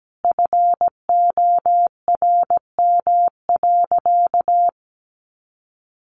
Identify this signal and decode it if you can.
Morse code, heard as a keyed tone, about 17 wpm, 700 Hz. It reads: FORM.
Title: Sentences from Top 1500 Words 17wpm